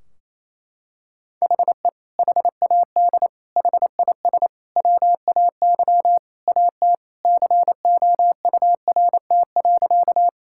Morse code, heard as a keyed tone, about 28 wpm, 700 Hz. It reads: HE HAD HIS WAY AT COURT.
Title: Sentences from Top 1600 Words 28wpm